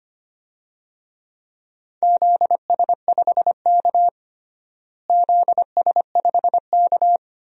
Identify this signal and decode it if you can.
Morse code, heard as a keyed tone, about 25 wpm, 700 Hz. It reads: ZS5K ZS5K
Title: DXCC-100 Call Signs - 4 Characters - Repeated 3x - 25wpm